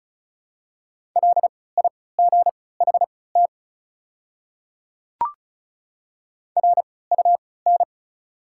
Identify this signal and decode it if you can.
Morse code, heard as a keyed tone, about 36 wpm, spaced as for 20 wpm, 690 Hz. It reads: LIGHT RUN
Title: 3x Character Spacing - Top 300 Words - 35wpm